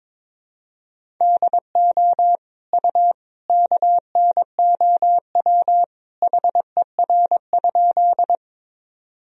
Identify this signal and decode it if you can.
Morse code, heard as a keyed tone, about 22 wpm, 700 Hz. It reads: DO U KNOW HER?
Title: Sentences from Top 100 Words and Abbreviations 22wpm